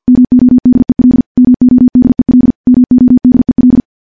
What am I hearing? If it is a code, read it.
Morse code, heard as a keyed tone, about 50 wpm, 250 Hz. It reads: MODEL MODEL MODEL
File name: morse / morse-training-50WPM